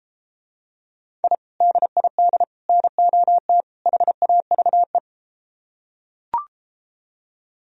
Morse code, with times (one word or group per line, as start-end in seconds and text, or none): 1.2-1.3 I
1.6-2.4 DID
2.7-3.6 NOT
3.9-5.0 HAVE